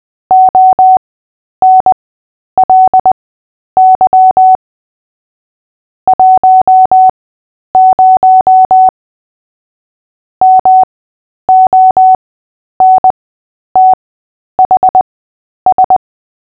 Morse code, heard as a keyed tone, about 20 wpm, spaced as for 10 wpm, 750 Hz. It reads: ONLY 10 MONTHS
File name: Short-Story-5-10wpm